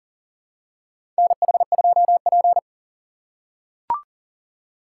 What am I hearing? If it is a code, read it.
Morse code, heard as a keyed tone, about 40 wpm, 700 Hz. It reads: NH2P
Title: USA Call Signs - 4 Characters - 4x Character Spacing - 40wpm